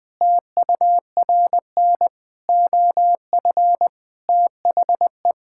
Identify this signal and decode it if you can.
Morse code, heard as a keyed tone, about 21 wpm, 700 Hz. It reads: TURN OF THE
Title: Sentences with 5-Word Phrases - Spoken Beginning - 20wpm